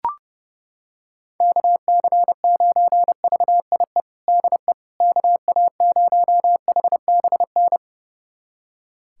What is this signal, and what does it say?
Morse code, heard as a keyed tone, about 32 wpm, 700 Hz. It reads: KC9VIE DE KA0HBN
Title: USA Callsign DE Callsign - 6 Characters - 30wpm